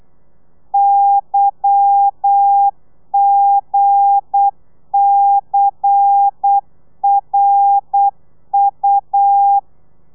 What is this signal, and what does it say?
Morse code, heard as a keyed tone, about 8 wpm, 800 Hz. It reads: YGCRU
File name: telegrafia / i2tfj / Lez10 35